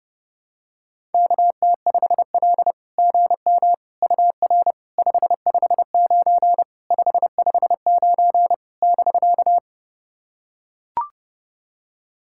Morse code, heard as a keyed tone, about 30 wpm, 700 Hz. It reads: KT5L GM UR 559 559 <BK>
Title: SOTA QSO Practice 30wpm